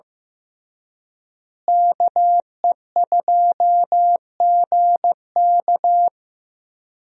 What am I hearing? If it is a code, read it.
Morse code, heard as a keyed tone, about 16 wpm, 700 Hz. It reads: KE2GK
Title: USA Call Signs - 5 Characters - Repeated 3x - 15wpm